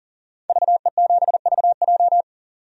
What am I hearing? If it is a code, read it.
Morse code, heard as a keyed tone, about 40 wpm, 700 Hz. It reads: VE7VJ
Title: DXCC-100 Call Signs - 5 Characters - 40wpm